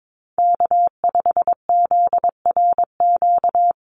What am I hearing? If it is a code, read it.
Morse code, heard as a keyed tone, about 22 wpm, 700 Hz. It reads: K5ZRQ
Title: USA Call Signs 22wpm